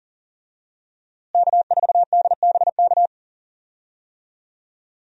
Morse code, heard as a keyed tone, about 40 wpm, 700 Hz. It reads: K4DBK
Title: USA Call Signs - 5 Characters - Repeated 2x - 40wpm